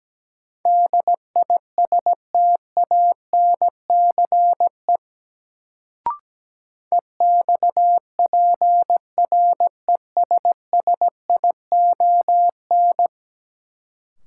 Morse code, heard as a keyed tone, about 17 wpm, 700 Hz. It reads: DISTANCE EXPRESSION
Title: Sentences from Top 1200 Words - New Word Drill Alt - 17wpm